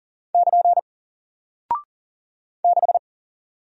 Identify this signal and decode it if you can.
Morse code, heard as a keyed tone, about 40 wpm, 700 Hz. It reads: <KN> 6
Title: Single Character (Letter:Number:Punctuation:Prosign) Alt 40wpm